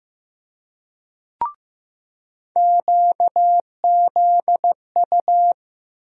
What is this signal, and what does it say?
Morse code, heard as a keyed tone, about 15 wpm, 700 Hz. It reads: QZU
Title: USA Call Sign Postfixes - 3 Letters 15wpm